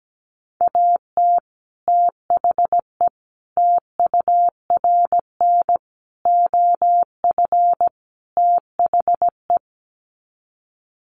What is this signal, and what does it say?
Morse code, heard as a keyed tone, about 17 wpm, 700 Hz. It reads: AT THE TURN OF THE
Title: Sets of 5 Words 17wpm